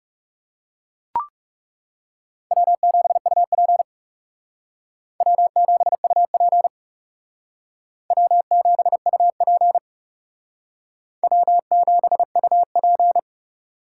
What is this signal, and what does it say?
Morse code, heard as a keyed tone, about 38 wpm, 700 Hz. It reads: W7UP W7UP W7UP W7UP